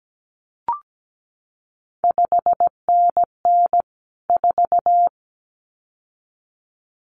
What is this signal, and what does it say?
Morse code, heard as a keyed tone, about 17 wpm, 700 Hz. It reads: 5NN 4
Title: CQ WW DX Contest Exchange - 17wpm